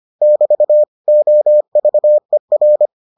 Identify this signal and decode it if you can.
Morse code, heard as a keyed tone, about 25 wpm, 600 Hz. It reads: <BT> OVER